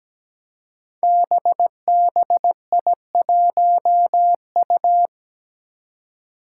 Morse code, heard as a keyed tone, about 17 wpm, 700 Hz. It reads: BBI1U
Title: Five Challenging Letter-Number Combinations 17wpm